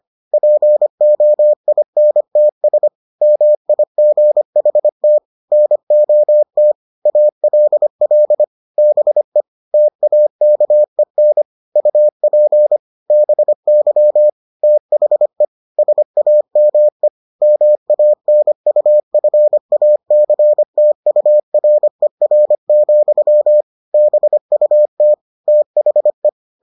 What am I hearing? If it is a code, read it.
Morse code, heard as a keyed tone, about 25 wpm, 600 Hz. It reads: POINTS MIGHT NOT ALL BE TAKEN UP BY THE SAME MANUFACTURER, BUT THE